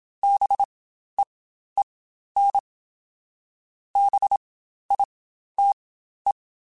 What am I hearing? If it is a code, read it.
Morse code, heard as a keyed tone, about 27 wpm, spaced as for 12 wpm, 800 Hz. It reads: BEEN BITE
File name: Spaced Alphas / Spaced Alphas mp3s / Bs sp @25